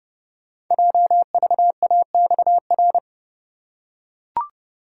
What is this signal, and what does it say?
Morse code, heard as a keyed tone, about 30 wpm, 700 Hz. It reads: JVAXR
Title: Code Groups - 5 Letter:Numbers - 30wpm